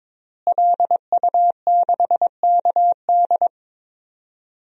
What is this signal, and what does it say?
Morse code, heard as a keyed tone, about 22 wpm, 700 Hz. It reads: LU6KD
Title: DXCC-100 Call Signs - 5 Characters - Repeated 2x - 22wpm